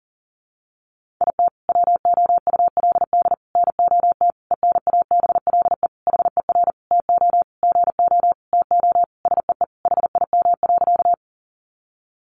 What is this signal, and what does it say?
Morse code, heard as a keyed tone, about 40 wpm, 700 Hz. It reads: IT WOULD NOT ENABLE HER TO GO TO SEE HIM.
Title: Sentences from Top 1500 Words 40wpm